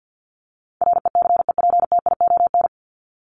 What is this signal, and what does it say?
Morse code, heard as a keyed tone, about 50 wpm, 700 Hz. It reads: RECEPTION
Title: Top 1751-2000 Words 50wpm